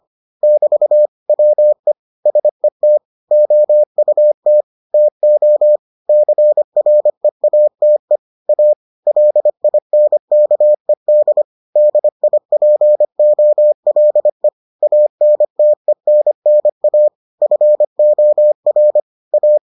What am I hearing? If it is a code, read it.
Morse code, heard as a keyed tone, about 25 wpm, 600 Hz. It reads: <BT> WE SET OUT TO CREATE A LINKED DIPOLE ANTENNA FOR A